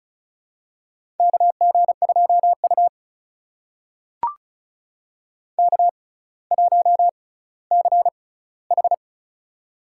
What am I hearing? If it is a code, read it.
Morse code, heard as a keyed tone, about 36 wpm, spaced as for 18 wpm, 700 Hz. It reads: KG2U K1CH
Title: USA Call Signs - 4 Characters - 6x Character Spacing - 35wpm